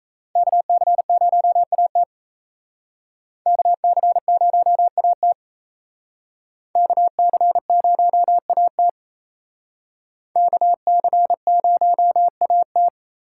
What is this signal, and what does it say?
Morse code, heard as a keyed tone, about 35 wpm, 700 Hz. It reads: KC0AT KC0AT KC0AT KC0AT